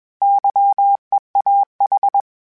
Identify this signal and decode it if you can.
Morse code, heard as a keyed tone, about 21 wpm, 800 Hz. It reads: YEAH
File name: Misc / Misc-mp3s / Misc @20